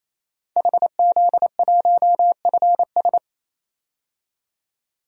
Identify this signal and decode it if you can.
Morse code, heard as a keyed tone, about 28 wpm, 700 Hz. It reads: HZ1FS
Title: DXCC-100 Call Signs - 5 Characters - 28wpm